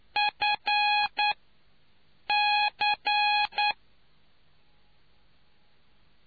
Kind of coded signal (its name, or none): Morse code